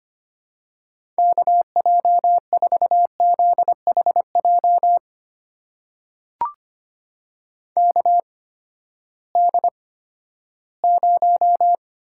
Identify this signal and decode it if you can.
Morse code, heard as a keyed tone, about 25 wpm, spaced as for 14 wpm, 700 Hz. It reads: KJ4ZHJ KD0
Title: USA Call Signs - 6 Characters - 8x Character Spacing - 25wpm